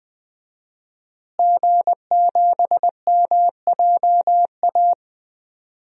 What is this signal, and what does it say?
Morse code, heard as a keyed tone, about 20 wpm, 700 Hz. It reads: G7MJA
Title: DXCC-100 Call Signs - 5 Characters - Repeated 2x - 20wpm